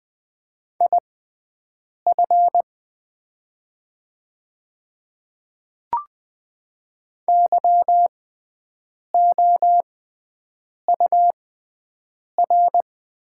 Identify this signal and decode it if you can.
Morse code, heard as a keyed tone, about 20 wpm, spaced as for 7 wpm, 700 Hz. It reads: IF YOUR
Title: 6x Character Spacing - Top 300 Words - 20wpm